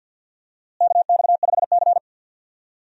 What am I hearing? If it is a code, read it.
Morse code, heard as a keyed tone, about 48 wpm, 700 Hz. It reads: KX5C
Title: USA Call Signs - 4 Characters - Repeated 2x - 50wpm